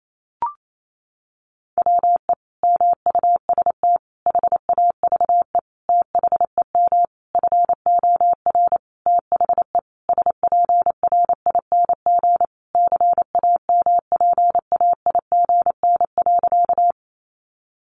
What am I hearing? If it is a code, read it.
Morse code, heard as a keyed tone, about 28 wpm, 700 Hz. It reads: WE MUST HAVE THEM FOR THE SPRING CAMPAIGN.
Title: Sentences from Top 1200 Words 28wpm